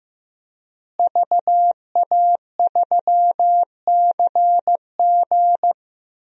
Morse code, heard as a keyed tone, about 15 wpm, 700 Hz. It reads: VA3CG
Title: DXCC-100 Call Signs - 5 Characters - Repeated 2x - 15wpm